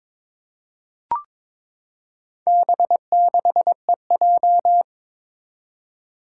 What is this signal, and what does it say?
Morse code, heard as a keyed tone, about 23 wpm, 700 Hz. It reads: B6EJ
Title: Four Challenging Letter-Number Combinations 22wpm